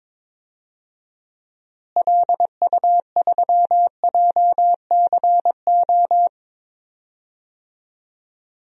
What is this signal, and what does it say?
Morse code, heard as a keyed tone, about 22 wpm, 700 Hz. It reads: LU3JCO
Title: DXCC-100 Call Signs - 6 Characters - Repeated 3x - 22wpm